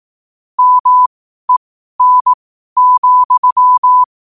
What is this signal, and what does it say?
Morse code, heard as a keyed tone, about 18 wpm, spaced as for 12 wpm, 1000 Hz. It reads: MEN,
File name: qotdfiles / 12wpm